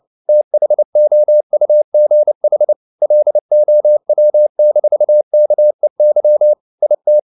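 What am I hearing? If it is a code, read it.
Morse code, heard as a keyed tone, about 29 wpm, 600 Hz. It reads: THOUGH LOW-KEY IT
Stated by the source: Morse code at 29 wpm, sending Ken Follett's novel Never in English, from CW med Gnister.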